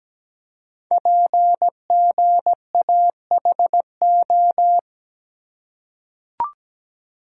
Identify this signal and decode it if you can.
Morse code, heard as a keyed tone, about 17 wpm, 700 Hz. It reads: PGAHO
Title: Code Groups - 5 Letters - 17wpm